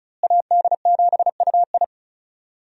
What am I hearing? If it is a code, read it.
Morse code, heard as a keyed tone, about 36 wpm, 700 Hz. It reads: AD7UI
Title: USA Call Signs - 5 Characters - Repeated 3x - 35wpm